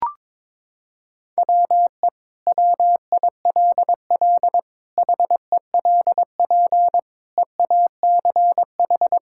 Morse code, heard as a keyed tone, about 22 wpm, 700 Hz. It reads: WE WILL HELP EACH
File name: practice / 22wpm / Sentences with 2-Word Phrases - 22wpm